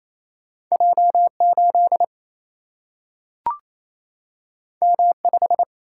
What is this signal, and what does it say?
Morse code, heard as a keyed tone, about 28 wpm, 700 Hz. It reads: J8 M5
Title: Single-Letter Single-Number 28wpm